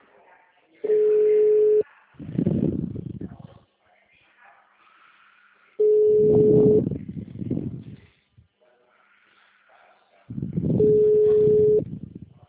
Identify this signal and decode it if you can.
Ringback tone (Europe, 425 Hz), with speech.
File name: out- -2011- - - .6